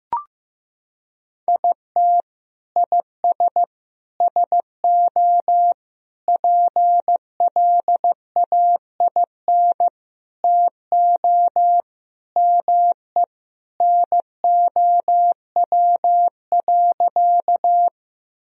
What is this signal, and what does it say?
Morse code, heard as a keyed tone, about 15 wpm, 700 Hz. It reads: IT IS SO PLAIN TO ME NOW.
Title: Sentences from Top 1000 Words - Review Part 2 - 15wpm